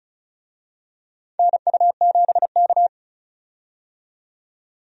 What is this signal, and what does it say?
Morse code, heard as a keyed tone, about 36 wpm, 700 Hz. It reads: NU7K
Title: USA Call Signs - 4 Characters - 35wpm